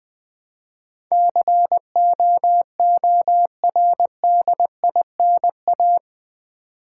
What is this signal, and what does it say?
Morse code, heard as a keyed tone, about 20 wpm, 700 Hz. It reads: COORDINA
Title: Build A Word - 4 Letters At A Time - 12-Letter Words - 20wpm